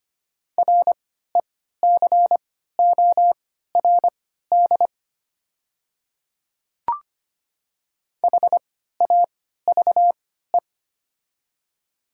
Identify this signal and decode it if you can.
Morse code, heard as a keyed tone, about 25 wpm, spaced as for 14 wpm, 700 Hz. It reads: RECORD HAVE
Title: 3x Character Spacing - Top 500 Words - 25wpm